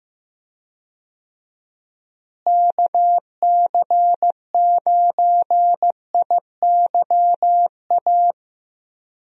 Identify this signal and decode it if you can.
Morse code, heard as a keyed tone, about 15 wpm, 700 Hz. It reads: KC9IYA
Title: USA Call Signs - 6 Characters - Repeated 3x - 15wpm